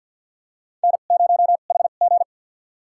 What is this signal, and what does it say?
Morse code, heard as a keyed tone, about 52 wpm, spaced as for 36 wpm, 700 Hz. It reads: N0HG
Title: USA Call Signs - 4 Characters - 2x Character Spacing - 50wpm